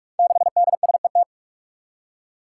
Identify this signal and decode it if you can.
Morse code, heard as a keyed tone, about 44 wpm, 700 Hz. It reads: 6DSET
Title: Five Challenging Letter-Number Combinations 45wpm